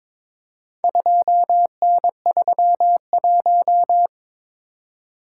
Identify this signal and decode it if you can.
Morse code, heard as a keyed tone, about 22 wpm, 700 Hz. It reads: 2N31
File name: practice / 22wpm / Four Letter-Number Combinations - 22wpm